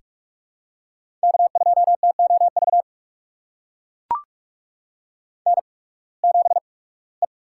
Morse code, heard as a keyed tone, about 43 wpm, spaced as for 23 wpm, 700 Hz. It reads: K2TOV N7E
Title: USA Call Signs - 5 Characters - 8x Character Spacing - 45wpm